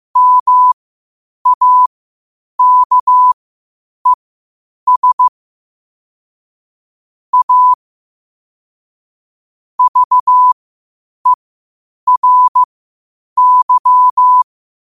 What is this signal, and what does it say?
Morse code, heard as a keyed tone, about 15 wpm, spaced as for 8 wpm, 1000 Hz. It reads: MAKES A VERY
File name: qotdfiles / 8wpm